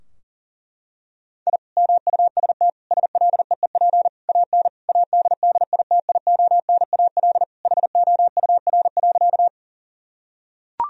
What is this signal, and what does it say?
Morse code, heard as a keyed tone, about 37 wpm, 700 Hz. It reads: I MUST SLEEP AN ADDITIONAL HOUR.
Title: Sentences from Top 1000 Words - Review Part 4 - 40wpm